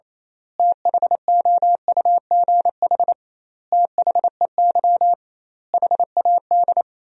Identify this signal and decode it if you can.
Morse code, heard as a keyed tone, about 27 wpm, 700 Hz. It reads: THOUGH THEY HAD
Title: 2x Word-Spacing - Sentences from Top 400 Words - 28wpm